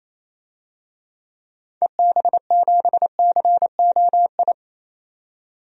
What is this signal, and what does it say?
Morse code, heard as a keyed tone, about 28 wpm, 700 Hz. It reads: EB7COI
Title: DXCC-100 Call Signs - 6 Characters - 28wpm